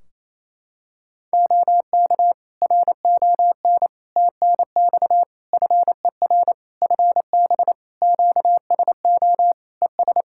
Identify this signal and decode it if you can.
Morse code, heard as a keyed tone, about 28 wpm, 700 Hz. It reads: OK RON TNX FER FB QSO ES
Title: Practice QSO - Exchange 4 - Ending - 28wpm